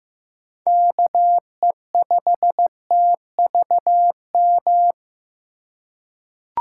Morse code, heard as a keyed tone, about 15 wpm, 700 Hz. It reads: KE5TVM
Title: USA Call Signs - 6 Characters - 3x Character Spacing - 15wpm